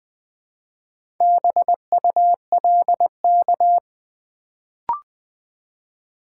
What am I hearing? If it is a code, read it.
Morse code, heard as a keyed tone, about 20 wpm, 700 Hz. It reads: BULK
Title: Build A Word - 1 Letter At A Time - 4-Letter Words - 20wpm